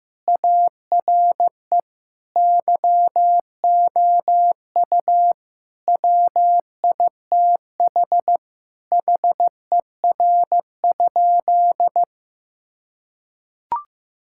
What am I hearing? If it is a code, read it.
Morse code, heard as a keyed tone, about 16 wpm, 700 Hz. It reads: ARE YOU WITH HER?
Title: Sentences from Top 1000 Words - Review Part 5 - 15wpm